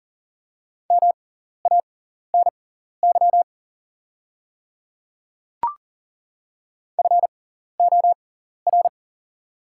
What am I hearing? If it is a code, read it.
Morse code, heard as a keyed tone, about 40 wpm, spaced as for 14 wpm, 690 Hz. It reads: MANY FOR
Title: 6x Character Spacing - Top 100 Words - 40wpm